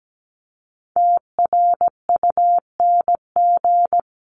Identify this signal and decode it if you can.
Morse code, heard as a keyed tone, about 17 wpm, 700 Hz. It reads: TRUNG
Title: CWT Exchange 17wpm